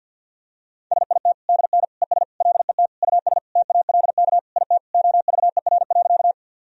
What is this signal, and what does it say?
Morse code, heard as a keyed tone, about 47 wpm, 700 Hz. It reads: SIT DN ES LET US TALK IT OVER.
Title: Sentences from Top 500 Words and Abbreviations 50wpm